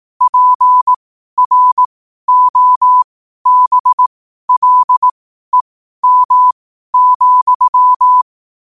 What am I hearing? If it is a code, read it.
Morse code, heard as a keyed tone, about 18 wpm, spaced as for 12 wpm, 1000 Hz. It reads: PROBLEM,
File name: qotdfiles / 12wpm